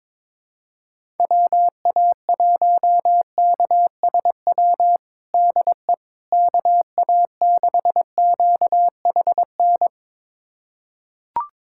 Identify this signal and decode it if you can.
Morse code, heard as a keyed tone, about 22 wpm, 700 Hz. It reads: WA1KSW DE KA6QHN
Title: USA Callsign DE Callsign - 6 Characters - 22wpm